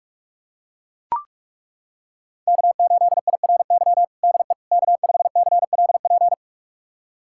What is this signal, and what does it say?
Morse code, heard as a keyed tone, about 42 wpm, 700 Hz. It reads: K8IRY DE K5CLP
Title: USA Callsign DE Callsign - 5 Characters - 45wpm